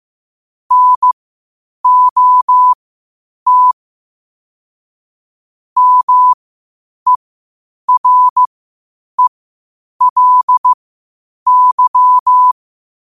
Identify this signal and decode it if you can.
Morse code, heard as a keyed tone, about 15 wpm, spaced as for 8 wpm, 1000 Hz. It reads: NOT MERELY
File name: qotdfiles / 8wpm